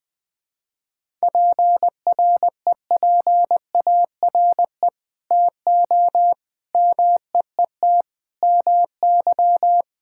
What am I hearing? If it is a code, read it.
Morse code, heard as a keyed tone, about 20 wpm, 700 Hz. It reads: PREPARE TO MEET MY
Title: Sentences from Top 1000 Words - Review Part 6 - 20wpm